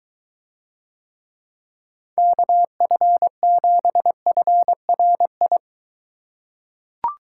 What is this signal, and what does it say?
Morse code, heard as a keyed tone, about 22 wpm, 700 Hz. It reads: KF7FRI